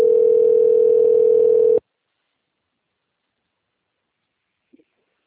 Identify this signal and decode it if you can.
Ringback tone (North America), 440 + 480 Hz.